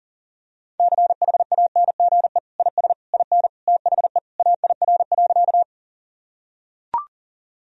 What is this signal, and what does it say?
Morse code, heard as a keyed tone, about 37 wpm, 700 Hz. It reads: CHANGE IS IN THE AIR.
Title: Sentences from Top 1000 Words - Review Part 1 - 40wpm